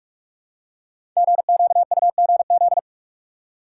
Morse code, heard as a keyed tone, about 44 wpm, 700 Hz. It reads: GQUGZ
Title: Code Groups - 5 Letters - 45wpm